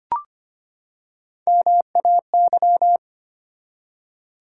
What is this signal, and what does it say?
Morse code, heard as a keyed tone, about 25 wpm, 700 Hz. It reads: MAY
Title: Three Letter Words - 25wpm